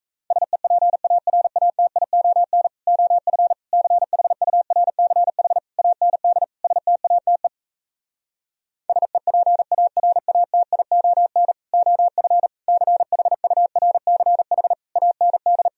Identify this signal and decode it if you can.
Morse code, heard as a keyed tone, about 38 wpm, 700 Hz. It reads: SEPARATION OF CHURCH AND STATE SEPARATION OF CHURCH AND